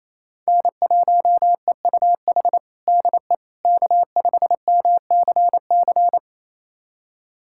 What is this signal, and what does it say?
Morse code, heard as a keyed tone, about 28 wpm, 700 Hz. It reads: N1EUH DE K5MCC
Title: USA Callsign DE Callsign - 5 Characters - 28wpm